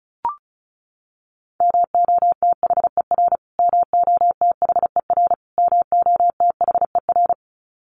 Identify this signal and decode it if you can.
Morse code, heard as a keyed tone, about 34 wpm, 700 Hz. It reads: MOTHER MOTHER MOTHER
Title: Top 500 Words Repeated 35wpm